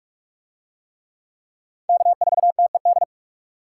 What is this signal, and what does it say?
Morse code, heard as a keyed tone, about 43 wpm, 700 Hz. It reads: K4TED